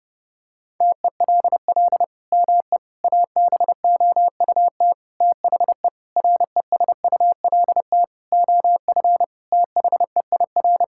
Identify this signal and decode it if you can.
Morse code, heard as a keyed tone, about 30 wpm, 700 Hz. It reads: TELL ME ABOUT THE RESULT OF THEIR
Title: Sentences from Top 1000 Words - Review Part 5 - 30wpm